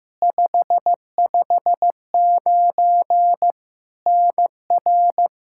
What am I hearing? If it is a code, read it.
Morse code, heard as a keyed tone, about 15 wpm, 700 Hz. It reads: 559 NR
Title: Practice QSO - Exchange 1 - Beginning - 15wpm